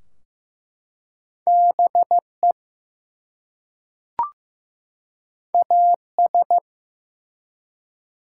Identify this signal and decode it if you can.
Morse code, heard as a keyed tone, about 15 wpm, 700 Hz. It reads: BE AS
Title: Two Letter Words - 15wpm